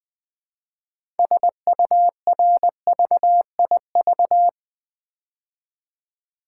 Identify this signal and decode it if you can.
Morse code, heard as a keyed tone, about 20 wpm, 700 Hz. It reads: SURVIV
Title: Build A Word - 2 Letters At A Time - 8-Letter Words - 20wpm